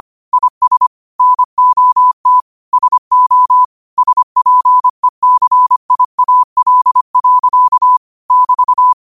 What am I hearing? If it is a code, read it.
Morse code, heard as a keyed tone, about 25 wpm, 1000 Hz. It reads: IS NOT SO SPECIAL. <BT>